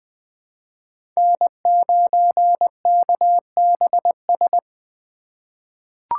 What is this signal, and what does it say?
Morse code, heard as a keyed tone, about 20 wpm, 700 Hz. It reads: N9KBS
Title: USA Call Signs - 2x Character Spacing - 20wpm